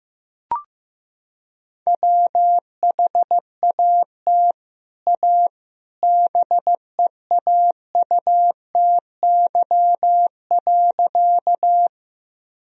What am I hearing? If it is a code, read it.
Morse code, heard as a keyed tone, about 15 wpm, 700 Hz. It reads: WHAT A BEAUTY.
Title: Sentences from Top 1000 Words - Review Part 2 - 15wpm